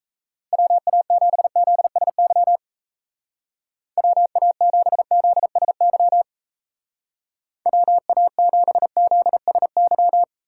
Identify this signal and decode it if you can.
Morse code, heard as a keyed tone, about 34 wpm, 700 Hz. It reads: WA7ZSY WA7ZSY WA7ZSY
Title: Speed Racing - USA 2x3 Call Signs 28wpm